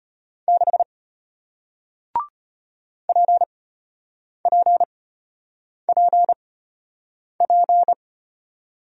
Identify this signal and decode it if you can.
Morse code, heard as a keyed tone, about 33 wpm, spaced as for 8 wpm, 700 Hz. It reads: 6 PPPP